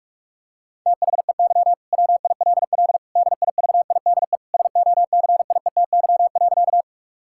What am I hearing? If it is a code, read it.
Morse code, heard as a keyed tone, about 43 wpm, 700 Hz. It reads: THEY WILL DIVIDE SOCIETY.
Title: Sentences from Top 1600 Words 45wpm